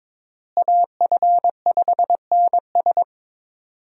Morse code, heard as a keyed tone, about 22 wpm, 700 Hz. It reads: AF5NS
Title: USA Call Signs - 5 Characters - Repeated 2x - 22wpm